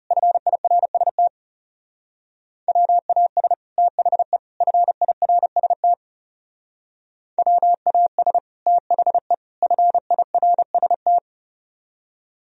Morse code, heard as a keyed tone, about 38 wpm, 700 Hz. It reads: FIRST WAS THE FIRST WAS THE FIRST
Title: Speed-Racing - Sets of 3 Words 30wpm